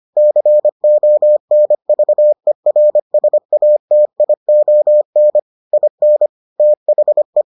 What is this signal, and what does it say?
Morse code, heard as a keyed tone, about 25 wpm, 600 Hz. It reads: CONVERSATION IN THE